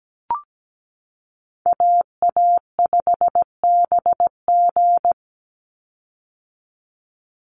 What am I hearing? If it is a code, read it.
Morse code, heard as a keyed tone, about 17 wpm, 700 Hz. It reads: AA5BG
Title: Call Signs - 17wpm